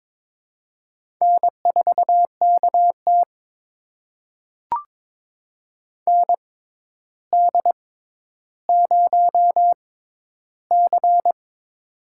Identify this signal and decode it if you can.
Morse code, heard as a keyed tone, about 22 wpm, spaced as for 11 wpm, 700 Hz. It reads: N4KT ND0C
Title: USA Call Signs - 4 Characters - 6x Character Spacing - 22wpm